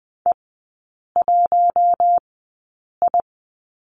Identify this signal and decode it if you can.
Morse code, heard as a keyed tone, about 20 wpm, spaced as for 8 wpm, 700 Hz. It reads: E1I
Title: 152 Lesson 25e - Letter p - Three Characters 20wpm